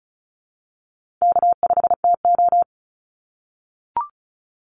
Morse code, heard as a keyed tone, about 34 wpm, 700 Hz. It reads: K5TO
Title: Call Signs - 35wpm